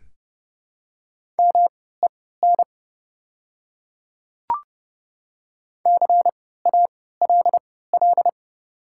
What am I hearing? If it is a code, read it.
Morse code, heard as a keyed tone, about 30 wpm, spaced as for 17 wpm, 690 Hz. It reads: MEN CALL
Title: 3x Character Spacing - Top 300 Words - 30wpm